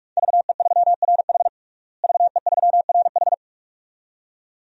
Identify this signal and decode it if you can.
Morse code, heard as a keyed tone, about 46 wpm, 700 Hz. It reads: VE3RH VE3RH
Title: DXCC-100 Call Signs - 5 Characters - Repeated 2x - 45wpm